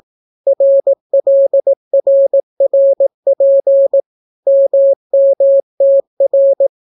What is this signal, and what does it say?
Morse code, heard as a keyed tone, about 18 wpm, 550 Hz. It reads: RLRRP MMTR